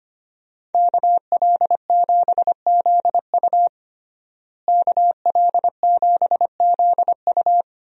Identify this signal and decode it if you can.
Morse code, heard as a keyed tone, about 25 wpm, 700 Hz. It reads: KL7ZU KL7ZU
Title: USA Call Signs - 5 Characters - Repeated 2x - 25wpm